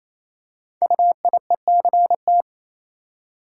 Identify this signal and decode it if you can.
Morse code, heard as a keyed tone, about 27 wpm, 700 Hz. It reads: UIECT
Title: Code Groups - 5 Letters - 28wpm